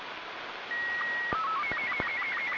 SSTV (slow-scan television), Robot 36.
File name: simple-file-list / ariss